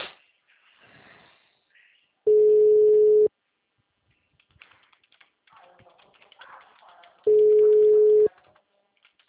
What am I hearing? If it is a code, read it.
Ringback tone (Europe, 425 Hz).
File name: out- -2016- - - .433